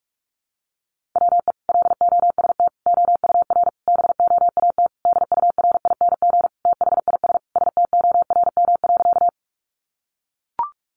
Morse code, heard as a keyed tone, about 44 wpm, 700 Hz. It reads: WE LOST OUR BOAT DURING THIS STORM.
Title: Sentences from Top 1500 Words 45wpm